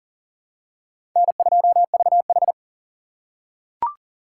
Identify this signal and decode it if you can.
Morse code, heard as a keyed tone, about 40 wpm, 700 Hz. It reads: N2VH
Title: USA Call Signs - 4 Characters - 4x Character Spacing - 40wpm